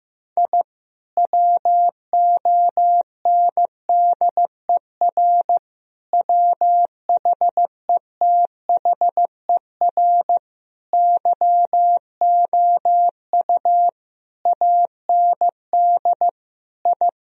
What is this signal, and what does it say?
Morse code, heard as a keyed tone, about 15 wpm, 700 Hz. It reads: I WONDER WHETHER YOU AND I
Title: Sentences from Top 1000 Words - Review Part 3 - 15wpm